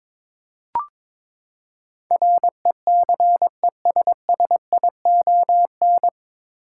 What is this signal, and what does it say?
Morse code, heard as a keyed tone, about 22 wpm, 700 Hz. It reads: RECESSION
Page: Morse Code Ninja practice 22wpm